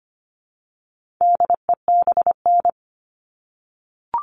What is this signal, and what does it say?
Morse code, heard as a keyed tone, about 25 wpm, 700 Hz. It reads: DEBN
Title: All Cut Numbers - 4-Digit 25wpm